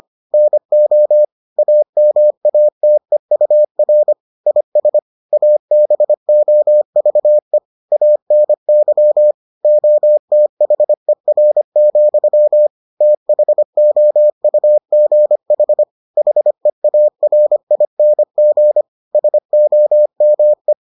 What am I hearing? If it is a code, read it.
Morse code, heard as a keyed tone, about 25 wpm, 600 Hz. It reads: NO AMATEUR IS ABOVE ANY OTHER, THOUGH HEARING SOME